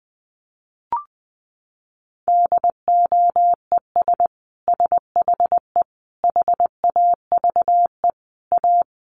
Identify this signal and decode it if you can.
Morse code, heard as a keyed tone, about 20 wpm, 700 Hz. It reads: DOES SHE HAVE A
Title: Sentences from Top 1500 Words 20wpm